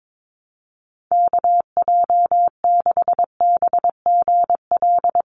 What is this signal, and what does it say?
Morse code, heard as a keyed tone, about 22 wpm, 700 Hz. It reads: KJ6BGL
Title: USA Call Signs 22wpm